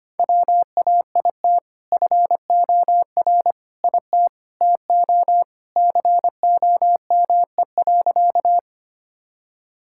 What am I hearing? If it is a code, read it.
Morse code, heard as a keyed tone, about 25 wpm, 700 Hz. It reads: WAIT FOR IT TO COME.
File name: practice / 25wpm / Sentences with 2-Word Phrases - 25wpm